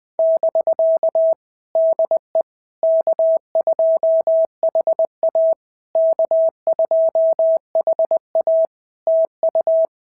Morse code, heard as a keyed tone, about 20 wpm, 650 Hz. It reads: <BK> DE K2HA K2HA TU
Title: POTA QSO Practice - 20wpm